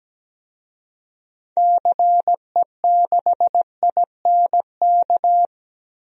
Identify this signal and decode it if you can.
Morse code, heard as a keyed tone, about 17 wpm, 700 Hz. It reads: CE6INK
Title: DXCC-100 Call Signs - 6 Characters - Repeated 2x - 17wpm